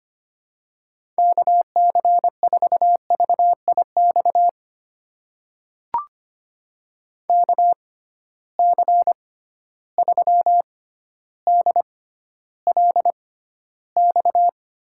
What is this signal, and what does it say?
Morse code, heard as a keyed tone, about 25 wpm, spaced as for 13 wpm, 700 Hz. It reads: KC4VIX KC3DLX
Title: USA Call Signs - 6 Characters - 6x Character Spacing - 25wpm